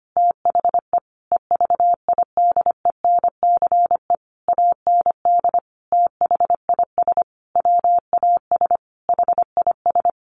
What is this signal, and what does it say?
Morse code, heard as a keyed tone, about 25 wpm, 700 Hz. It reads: THE EVIDENCE AND THIS WAS HIS
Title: A Sound Of Thunder 25wpm